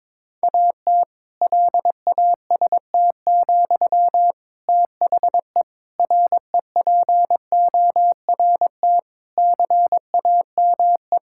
Morse code, heard as a keyed tone, about 21 wpm, 700 Hz. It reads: AT LAST, THE REPORT CAME
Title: Sentences from Top 1000 Words - Review Part 6 - 22wpm